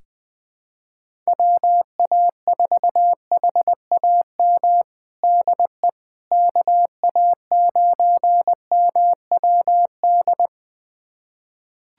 Morse code, heard as a keyed tone, about 20 wpm, 700 Hz. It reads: WA4HAM DE KA9MWD
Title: USA Callsign DE Callsign - 6 Characters - 20wpm